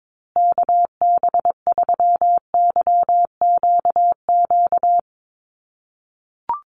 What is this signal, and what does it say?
Morse code, heard as a keyed tone, about 22 wpm, 700 Hz. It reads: KB3YQQ
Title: USA Call Signs Repeated 3x 22wpm